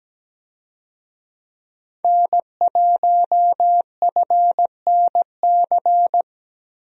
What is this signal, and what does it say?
Morse code, heard as a keyed tone, about 17 wpm, 700 Hz. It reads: N1FNC
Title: USA Call Signs - 5 Characters - 17wpm